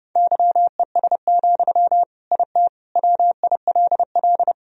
Morse code, heard as a keyed tone, about 32 wpm, 700 Hz. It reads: YES, IT WILL
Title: Sentences from Top 1000 Words - Review Part 1 - 30wpm